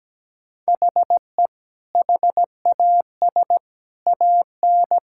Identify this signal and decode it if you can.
Morse code, heard as a keyed tone, about 17 wpm, 700 Hz. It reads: HE HAS AN
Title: Sentences from Top 1000 Words - Review Part 6 - 17wpm